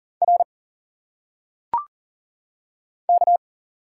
Morse code, heard as a keyed tone, about 40 wpm, 700 Hz. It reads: R K
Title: Single Letters - Phonetic Pronunciation 40wpm